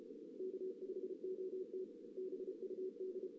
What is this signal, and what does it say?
Morse code, heard as a keyed tone, about 32 wpm, 410 Hz. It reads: K5OT BUD
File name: 2016 JAN CW NAQP / 2016 Jan naqp k5ot - 0-2